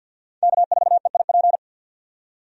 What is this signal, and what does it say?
Morse code, heard as a keyed tone, about 51 wpm, 700 Hz. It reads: K4EIP
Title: USA Call Signs - 5 Characters - 50wpm